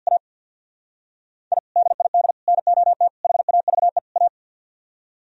Morse code, heard as a keyed tone, about 64 wpm, 700 Hz. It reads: A I DID NOT HAVE A